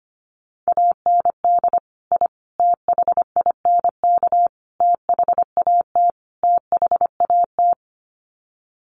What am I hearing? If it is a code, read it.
Morse code, heard as a keyed tone, about 25 wpm, 700 Hz. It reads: AND I THINK THAT THAT
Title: Sets of 5 Words Encore 25wpm